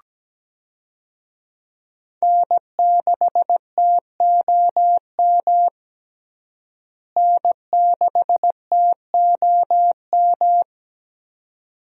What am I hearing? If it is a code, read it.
Morse code, heard as a keyed tone, about 17 wpm, 700 Hz. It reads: N6TOM N6TOM
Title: USA Call Signs - 5 Characters - Repeated 2x - 17wpm